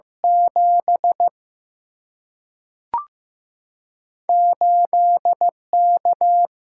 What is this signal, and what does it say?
Morse code, heard as a keyed tone, about 15 wpm, 700 Hz. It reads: 7 8 K
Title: Single-Letter Single-Number 15wpm